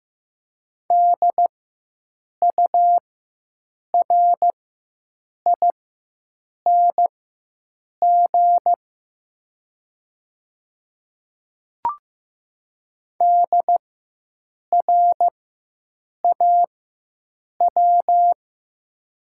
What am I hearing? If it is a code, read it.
Morse code, heard as a keyed tone, about 15 wpm, spaced as for 7 wpm, 700 Hz. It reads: DURING DRAW
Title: 4x Character Spacing - Top 500 Words - 15wpm